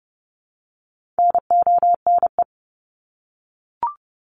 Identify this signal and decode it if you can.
Morse code, heard as a keyed tone, about 30 wpm, 700 Hz. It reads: NONE
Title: Top 1501-1750 Words 30wpm